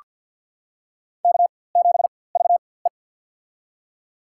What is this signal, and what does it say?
Morse code, heard as a keyed tone, about 47 wpm, spaced as for 23 wpm, 700 Hz. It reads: K7VE
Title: USA Call Signs - 4x Character Spacing - 50wpm